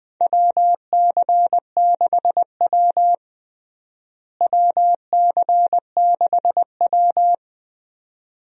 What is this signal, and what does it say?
Morse code, heard as a keyed tone, about 20 wpm, 700 Hz. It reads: WC6W WC6W
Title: USA Call Signs - 4 Characters - Repeated 2x - 20wpm